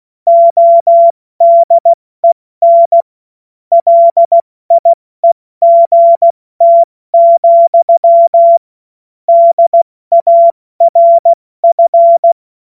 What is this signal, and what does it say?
Morse code, heard as a keyed tone, about 16 wpm, 680 Hz. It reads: ODEN LIEGT, DARF
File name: Morsekurs / Witze / Witz 17. August 2021, 70 F80